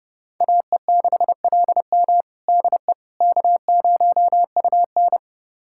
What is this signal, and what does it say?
Morse code, heard as a keyed tone, about 30 wpm, 700 Hz. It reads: AE6LM DE K0UN
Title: USA Callsign DE Callsign 30wpm